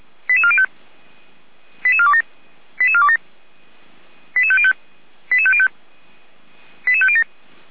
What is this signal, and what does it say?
Five-tone selective call (ZVEI), heard as quick runs of five tones, tones of about 70 ms. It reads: 88405 88427 88427 88505 88505 88507
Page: /455/SDR/ZVEI/88447